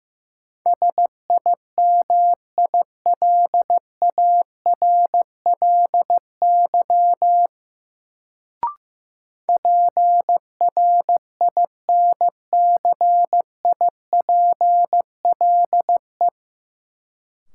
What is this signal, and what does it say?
Morse code, heard as a keyed tone, about 15 wpm, 700 Hz. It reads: SIMILARLY PRINCIPLE
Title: Sentences from Top 1400 Words - New Word Drill Alt - 15wpm